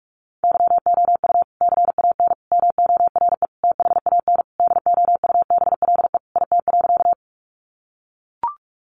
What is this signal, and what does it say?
Morse code, heard as a keyed tone, about 44 wpm, 700 Hz. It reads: YOU CAN MORE THAN DOUBLE IT.
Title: Sentences from Top 1400 Words 45wpm